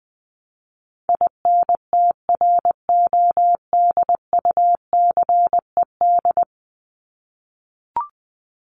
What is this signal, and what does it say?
Morse code, heard as a keyed tone, about 20 wpm, 700 Hz. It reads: INTRODUCED
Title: ED Suffix - Top 100 Words with Base Words 20wpm